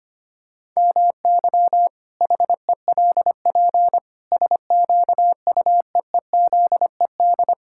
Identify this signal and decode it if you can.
Morse code, heard as a keyed tone, about 25 wpm, 700 Hz. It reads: MY HELP SQUEEZED
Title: Pangram Challenge v2 - 25wpm